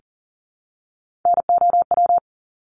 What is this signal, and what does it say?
Morse code, heard as a keyed tone, about 40 wpm, 700 Hz. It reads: NOW
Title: Top 200 Words 40wpm